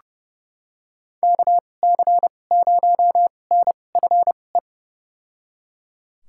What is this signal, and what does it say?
Morse code, heard as a keyed tone, about 30 wpm, spaced as for 22 wpm, 700 Hz. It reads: KC0NFE
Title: USA Call Signs - 6 Characters - 2x Character Spacing - 30wpm